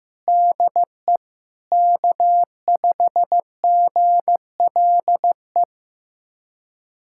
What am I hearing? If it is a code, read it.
Morse code, heard as a keyed tone, about 16 wpm, 700 Hz. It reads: DE K5GLE
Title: USA Callsign DE Callsign 15wpm